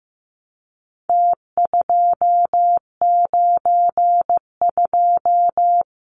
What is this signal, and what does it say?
Morse code, heard as a keyed tone, about 15 wpm, 700 Hz. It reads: T292
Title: Numbers - Contest Style - 4-Digit 15wpm